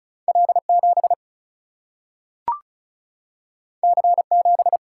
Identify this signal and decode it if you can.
Morse code, heard as a keyed tone, about 36 wpm, 700 Hz. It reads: L7 C7
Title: Single-Letter Single-Number 35wpm